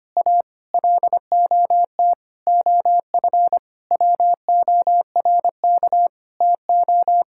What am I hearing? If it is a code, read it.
Morse code, heard as a keyed tone, about 25 wpm, 700 Hz. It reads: A LOT OF WORK TO
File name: practice / 25wpm / Sentences with 9-Word Phrases - 25wpm